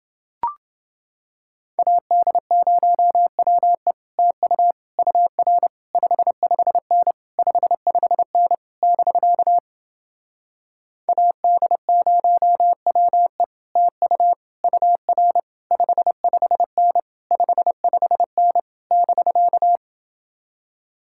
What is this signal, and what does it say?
Morse code, heard as a keyed tone, about 30 wpm, 700 Hz. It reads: AD0WE TU UR 55N 55N <BK> AD0WE TU UR 55N 55N <BK>